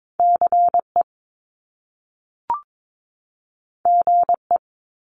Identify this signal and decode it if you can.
Morse code, heard as a keyed tone, about 22 wpm, 700 Hz. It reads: CE GE
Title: Top Two Letter Combinations in Words 22wpm